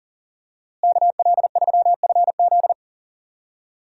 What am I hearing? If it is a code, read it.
Morse code, heard as a keyed tone, about 39 wpm, 700 Hz. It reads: KL3FZ
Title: USA Call Signs - 5 Characters - Repeated 3x - 40wpm